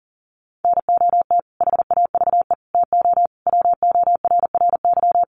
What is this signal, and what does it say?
Morse code, heard as a keyed tone, about 40 wpm, 700 Hz. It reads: NOT HAVE TO WORRY
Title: Sets of 5 Words 40wpm